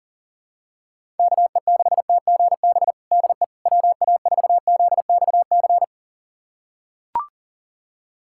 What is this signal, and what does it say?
Morse code, heard as a keyed tone, about 40 wpm, 700 Hz. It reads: KE6TGB DE WA4ZXC
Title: USA Callsign DE Callsign 40wpm